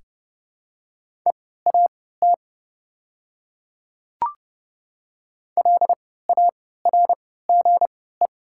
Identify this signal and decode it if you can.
Morse code, heard as a keyed tone, about 30 wpm, spaced as for 17 wpm, 700 Hz. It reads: EAT LARGE
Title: 3x Character Spacing - Top 300 Words - 30wpm